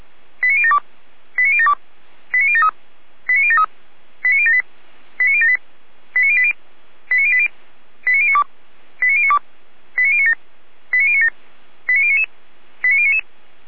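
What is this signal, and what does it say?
Five-tone selective call (ZVEI), heard as quick runs of five tones, tones of about 70 ms. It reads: 89072 89072 89073 89073 89078 89078 89080 89080 89092 89092 89097 89097 89099 89099
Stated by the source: Five-tone ZVEI selective call, received with an SDR.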